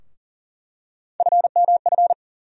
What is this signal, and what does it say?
Morse code, heard as a keyed tone, about 40 wpm, 700 Hz. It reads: FMF
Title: USA Call Sign Postfixes - 3 Letters 40wpm